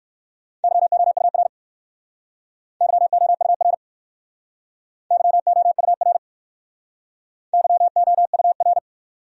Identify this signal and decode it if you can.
Morse code, heard as a keyed tone, about 59 wpm, 700 Hz. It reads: YOUR YOUR YOUR YOUR